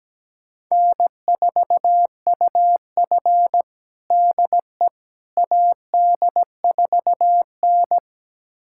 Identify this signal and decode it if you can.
Morse code, heard as a keyed tone, about 17 wpm, 700 Hz. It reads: N4UF DE AD4N
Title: USA Callsign DE Callsign - 4 Characters - 17wpm